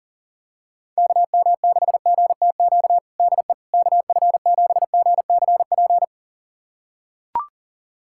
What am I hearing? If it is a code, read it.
Morse code, heard as a keyed tone, about 40 wpm, 700 Hz. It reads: KM6GTQ DE KF7GCP
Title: USA Callsign DE Callsign - 6 Characters - 40wpm